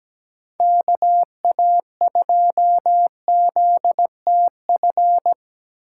Morse code, heard as a keyed tone, about 17 wpm, 700 Hz. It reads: KA2ZTF
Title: USA Call Signs - 2x Character Spacing - 17wpm